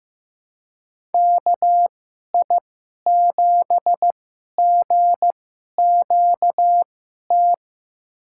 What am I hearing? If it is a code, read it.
Morse code, heard as a keyed tone, about 15 wpm, spaced as for 11 wpm, 700 Hz. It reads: KI7GQT
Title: USA Call Signs - 6 Characters - 2x Character Spacing - 15wpm